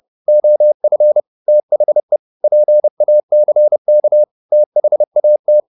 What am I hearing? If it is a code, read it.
Morse code, heard as a keyed tone, about 30 wpm, 600 Hz. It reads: OF THE PACK THAT